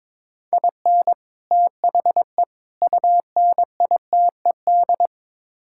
Morse code, heard as a keyed tone, about 23 wpm, 700 Hz. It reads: IN THE UNITED
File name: practice / 22wpm / Build A Phrase - 5-Word Phrases - 22wpm